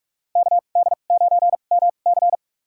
Morse code, heard as a keyed tone, about 46 wpm, spaced as for 32 wpm, 700 Hz. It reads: KD9MC
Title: USA Call Signs - 5 Characters - 2x Character Spacing - 45wpm